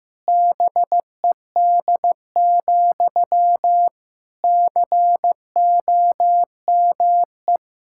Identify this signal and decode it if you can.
Morse code, heard as a keyed tone, about 15 wpm, 700 Hz. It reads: BED, COME
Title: Sentences from Top 1600 Words 15wpm